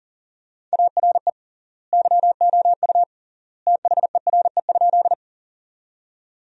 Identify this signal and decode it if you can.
Morse code, heard as a keyed tone, about 40 wpm, 700 Hz. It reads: ARE YOU THERE?
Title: 3x Word-Spacing - Sentences from Top 100 Words - 40wpm